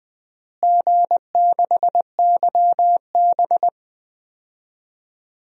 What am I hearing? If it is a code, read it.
Morse code, heard as a keyed tone, about 20 wpm, 700 Hz. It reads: G6YB
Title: DXCC-100 Call Signs - 4 Characters - Repeated 3x - 20wpm